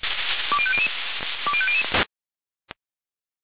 Five-tone selective call (ZVEI), heard as quick runs of five tones, tones of about 70 ms. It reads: 20500 20500